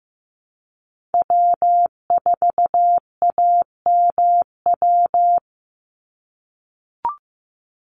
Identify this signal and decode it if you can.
Morse code, heard as a keyed tone, about 15 wpm, 700 Hz. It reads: W4AMW
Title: NA SOTA Chaser Callsigns 15wpm